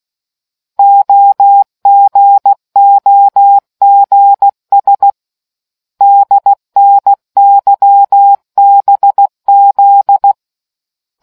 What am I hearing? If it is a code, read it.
Morse code, heard as a keyed tone, about 16 wpm, 800 Hz. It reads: OGOGS DNYBZ